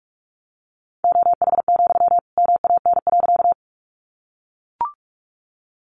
Morse code, heard as a keyed tone, about 44 wpm, 700 Hz. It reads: OH, MAN.
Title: Sentences from Top 500 Words 45wpm